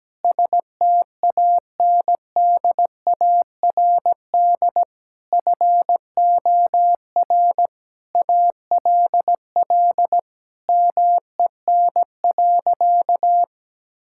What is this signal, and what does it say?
Morse code, heard as a keyed tone, about 17 wpm, 700 Hz. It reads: STANDARD FOR ALL MEN.
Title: Sentences from Top 1000 Words - Review Part 6 - 17wpm